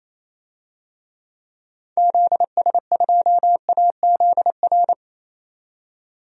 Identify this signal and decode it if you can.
Morse code, heard as a keyed tone, about 27 wpm, 700 Hz. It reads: ZS2AZR
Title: DXCC-100 Call Signs - 6 Characters - 28wpm